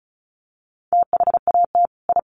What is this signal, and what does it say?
Morse code, heard as a keyed tone, about 34 wpm, 700 Hz. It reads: THAT I
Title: Sets of 2 Words 35wpm